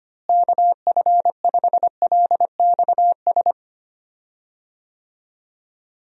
Morse code, heard as a keyed tone, about 25 wpm, 700 Hz. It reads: KF5LXS
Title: USA Call Signs - 6 Characters - Repeated 2x - 25wpm